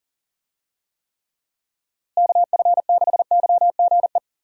Morse code, heard as a keyed tone, about 40 wpm, 700 Hz. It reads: KF6YGE
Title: USA Call Signs - 6 Characters - Repeated 2x - 40wpm